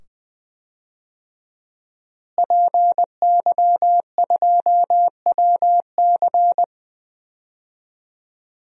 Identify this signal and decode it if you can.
Morse code, heard as a keyed tone, about 20 wpm, 700 Hz. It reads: PY2WC
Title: DXCC-100 Call Signs - 5 Characters - Repeated 2x - 20wpm